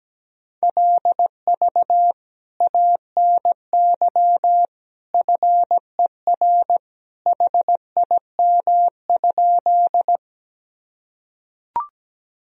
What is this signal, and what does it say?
Morse code, heard as a keyed tone, about 17 wpm, 700 Hz. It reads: LV ANY FER HIM?
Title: Sentences from Top 400 Words and Abbreviations 17wpm